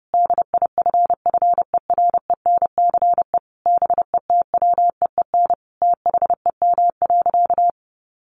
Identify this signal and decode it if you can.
Morse code, heard as a keyed tone, about 30 wpm, 700 Hz. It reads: DIFFERENCE BETWEEN THEM.
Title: Sentences from Top 800 Words 30wpm